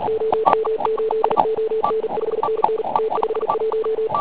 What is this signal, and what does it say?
Morse code, heard as a keyed tone, about 38 wpm, 440 Hz, with noise. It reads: OM8ON 5NN T51
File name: Pile UP CUC204